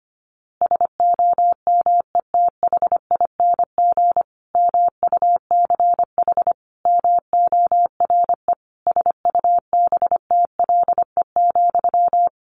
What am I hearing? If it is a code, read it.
Morse code, heard as a keyed tone, about 25 wpm, 700 Hz. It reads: SOMETHING MUCH MORE SUBTLE,
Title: A Sound Of Thunder 25wpm